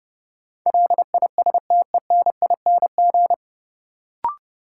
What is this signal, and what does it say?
Morse code, heard as a keyed tone, about 30 wpm, 700 Hz. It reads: LISTENING
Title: Sets of 5 Words - New Word Drill - 30wpm